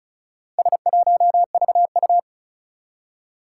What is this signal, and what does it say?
Morse code, heard as a keyed tone, about 36 wpm, 700 Hz. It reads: S1VU
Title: Four Challenging Letter-Number Combinations 35wpm